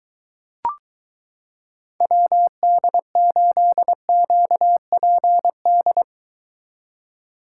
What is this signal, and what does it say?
Morse code, heard as a keyed tone, about 22 wpm, 700 Hz. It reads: WD8QPD